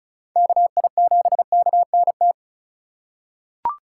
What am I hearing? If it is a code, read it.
Morse code, heard as a keyed tone, about 36 wpm, 700 Hz. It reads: KI7KNT
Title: USA Call Signs - 6 Characters - 4x Character Spacing - 35wpm